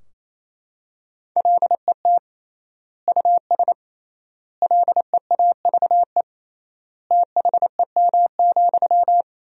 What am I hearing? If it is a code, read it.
Morse code, heard as a keyed tone, about 28 wpm, 700 Hz. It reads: LET US LEAVE THEM,
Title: 3x Word-Spacing - Sentences from Top 400 Words - 28wpm